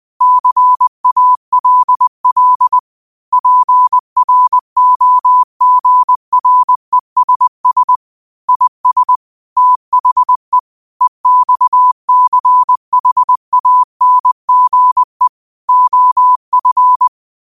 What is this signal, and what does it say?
Morse code, heard as a keyed tone, about 20 wpm, 1000 Hz. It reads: CALL PROGRESS IS THE EXCHANGE OF